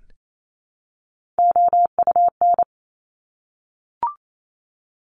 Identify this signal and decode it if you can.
Morse code, heard as a keyed tone, about 28 wpm, 700 Hz. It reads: OUN
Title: Top Three Letter Combinations in Words 28wpm